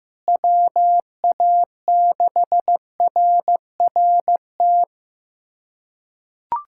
Morse code, heard as a keyed tone, about 15 wpm, 700 Hz. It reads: WA6RRT
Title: USA Callsign DE Callsign - 6 Characters - 15wpm